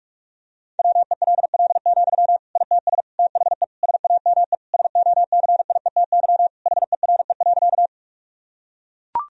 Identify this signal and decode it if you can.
Morse code, heard as a keyed tone, about 41 wpm, 700 Hz. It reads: WELL, ITS THE SAME SOCIETY HERE.
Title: Sentences from Top 1000 Words - Review Part 4 - 45wpm